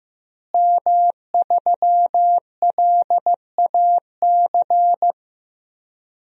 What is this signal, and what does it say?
Morse code, heard as a keyed tone, about 15 wpm, 700 Hz. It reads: M3LAC
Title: DXCC-100 Call Signs - 5 Characters - Repeated 3x - 15wpm